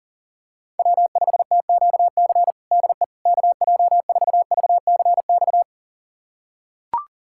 Morse code, heard as a keyed tone, about 40 wpm, 700 Hz. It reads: W5TQC DE KJ4VCX
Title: USA Callsign DE Callsign 40wpm